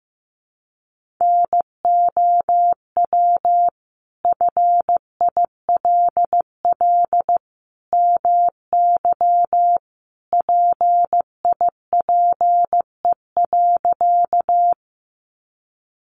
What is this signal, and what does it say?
Morse code, heard as a keyed tone, about 15 wpm, 700 Hz. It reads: NOW FILL MY PIPE.
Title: Sentences from Top 1500 Words 15wpm